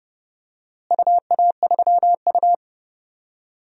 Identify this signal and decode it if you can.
Morse code, heard as a keyed tone, about 30 wpm, 700 Hz. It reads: UA3U
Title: DXCC-100 Call Signs - 4 Characters - Repeated 2x - 30wpm